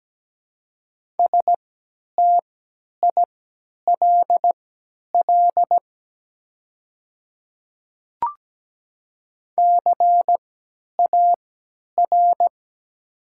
Morse code, heard as a keyed tone, about 17 wpm, spaced as for 10 wpm, 690 Hz. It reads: STILL CAR
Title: 3x Character Spacing - Top 300 Words - 17wpm